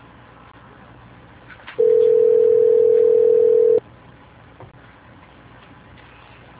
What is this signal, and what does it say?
Ringback tone (North America), 440 + 480 Hz.
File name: out- -2051- - - .14